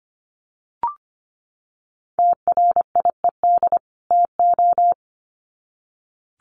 Morse code, heard as a keyed tone, about 25 wpm, 700 Hz. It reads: TRIED TO
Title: Sets of 2 Words 25wpm